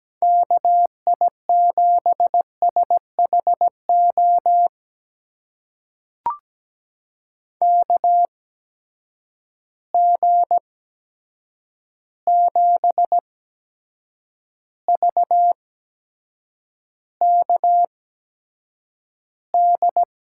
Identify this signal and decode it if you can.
Morse code, heard as a keyed tone, about 17 wpm, spaced as for 7 wpm, 700 Hz. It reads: KI7SHO KG7VKD
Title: USA Call Signs - 6 Characters - 8x Character Spacing - 17wpm